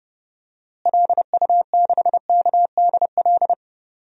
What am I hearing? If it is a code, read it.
Morse code, heard as a keyed tone, about 31 wpm, 700 Hz. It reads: LU6KDL
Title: DXCC-100 Call Signs - 6 Characters - Repeated 3x - 30wpm